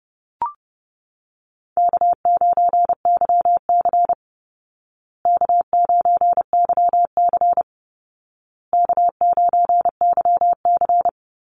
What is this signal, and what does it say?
Morse code, heard as a keyed tone, about 30 wpm, 700 Hz. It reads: K9YC K9YC K9YC
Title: Easy Call Signs Repeated 3x - 30wpm